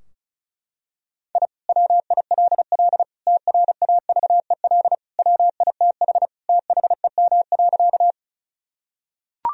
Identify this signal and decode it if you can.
Morse code, heard as a keyed tone, about 34 wpm, 700 Hz. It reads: I WILL TRAVEL WITH THEM.
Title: Sentences from Top 1000 Words - Review Part 2 - 35wpm